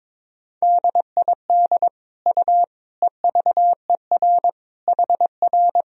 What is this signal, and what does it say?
Morse code, heard as a keyed tone, about 22 wpm, 700 Hz. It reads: DID U EVER HR
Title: Sentences from Top 400 Words and Abbreviations 22wpm